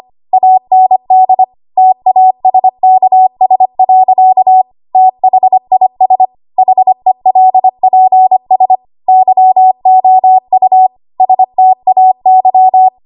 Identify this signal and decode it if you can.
Morse code, heard as a keyed tone, about 25 wpm, 750 Hz. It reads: AND TASKS. THIS HELPS YOU STAY
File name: Morse / 25WPM